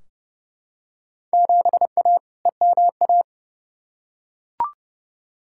Morse code, heard as a keyed tone, about 32 wpm, 700 Hz. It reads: 7A EMA
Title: Field Day Exchanges Encore - 30wpm